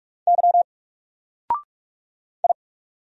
Morse code, heard as a keyed tone, about 46 wpm, 700 Hz. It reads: Y I
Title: Single Letter-Number 45wpm